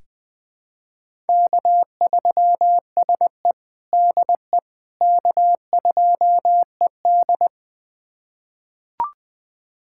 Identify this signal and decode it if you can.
Morse code, heard as a keyed tone, about 20 wpm, 700 Hz. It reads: K3SE DE K2ED
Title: USA Callsign DE Callsign - 4 Characters - 20wpm